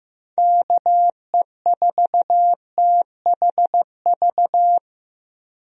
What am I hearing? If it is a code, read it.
Morse code, heard as a keyed tone, about 15 wpm, 700 Hz. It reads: KE4THV
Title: USA Call Signs - 6 Characters - 15wpm